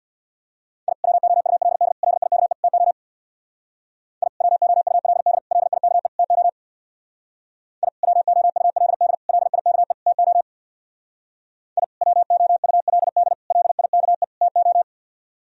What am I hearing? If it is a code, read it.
Morse code, heard as a keyed tone, about 62 wpm, 700 Hz. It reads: I WOULD LIKE TO I WOULD LIKE TO I WOULD LIKE TO I WOULD LIKE TO
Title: Speed-Racing - Sets of 4 Words 50wpm